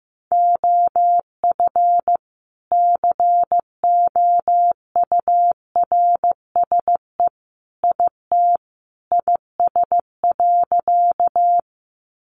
Morse code, heard as a keyed tone, about 15 wpm, 700 Hz. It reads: OF COURSE IT IS.
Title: Sentences from Top 500 Words 15wpm